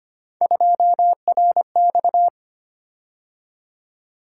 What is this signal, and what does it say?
Morse code, heard as a keyed tone, about 25 wpm, 700 Hz. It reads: 2RX
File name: Three Letter-Number Combinations - 25wpm